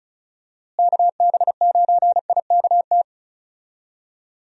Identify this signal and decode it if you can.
Morse code, heard as a keyed tone, about 34 wpm, 700 Hz. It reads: KB9IKT
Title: USA Call Signs - 6 Characters - Repeated 3x - 35wpm